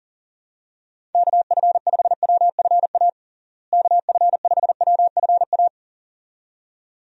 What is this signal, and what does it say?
Morse code, heard as a keyed tone, about 40 wpm, 700 Hz. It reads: KF5WFA KF5WFA
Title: USA Call Signs - 6 Characters - Repeated 2x - 40wpm